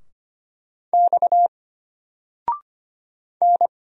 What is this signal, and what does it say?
Morse code, heard as a keyed tone, about 25 wpm, 700 Hz. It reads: X N
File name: Single-Letter-Number-25wpm